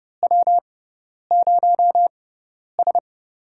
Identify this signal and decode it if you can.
Morse code, heard as a keyed tone, about 30 wpm, spaced as for 10 wpm, 700 Hz. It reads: W0S
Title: USA Call Signs - 6x Character Spacing - 30wpm